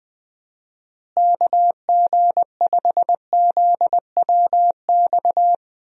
Morse code, heard as a keyed tone, about 21 wpm, 700 Hz. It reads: KG5ZWX
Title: USA Call Signs - 6 Characters - Repeated 2x - 20wpm